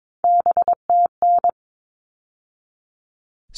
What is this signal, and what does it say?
Morse code, heard as a keyed tone, about 22 wpm, 700 Hz. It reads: BTN
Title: All Cut Numbers - 3-Digit 22wpm